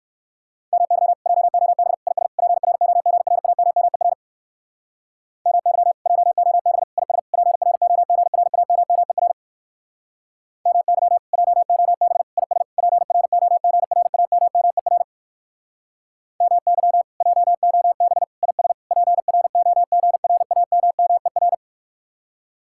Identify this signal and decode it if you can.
Morse code, heard as a keyed tone, about 58 wpm, 700 Hz. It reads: MY JOB IS PROGRAMMER MY JOB IS PROGRAMMER MY JOB IS PROGRAMMER MY JOB IS PROGRAMMER